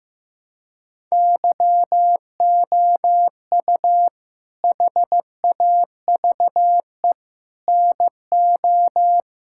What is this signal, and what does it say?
Morse code, heard as a keyed tone, about 16 wpm, 700 Hz. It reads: YOU HAVE NO
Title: Sentences from Top 1000 Words - Review Part 5 - 15wpm